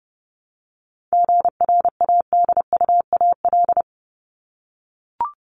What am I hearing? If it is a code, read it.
Morse code, heard as a keyed tone, about 30 wpm, 700 Hz. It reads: GRADUAL
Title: LY Suffix - Top 100 Words with Base Words 30wpm